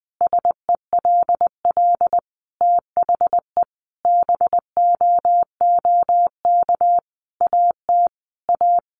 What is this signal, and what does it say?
Morse code, heard as a keyed tone, about 20 wpm, 700 Hz. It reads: SELL THE BOOK AT A
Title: Sentences from Top 1200 Words 20wpm